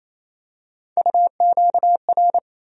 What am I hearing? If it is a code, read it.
Morse code, heard as a keyed tone, about 27 wpm, 700 Hz. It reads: UQR
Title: USA Call Sign Postfixes - 3 Letters 28wpm